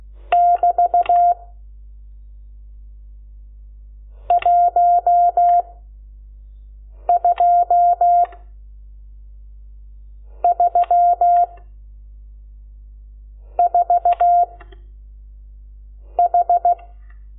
Morse code, heard as a keyed tone, about 16 wpm, 700 Hz. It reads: <BT> 1 2 3 4 H
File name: Lecciones / CLASE Nº12